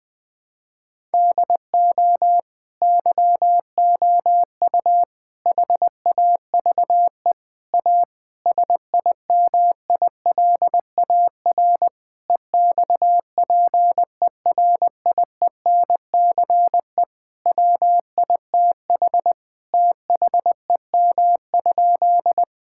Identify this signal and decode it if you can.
Morse code, heard as a keyed tone, about 21 wpm, 700 Hz. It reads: DO YOU HAVE A SIMILAR EXPERIENCE WITH THEM?
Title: Sentences from Top 1000 Words - Review Part 5 - 20wpm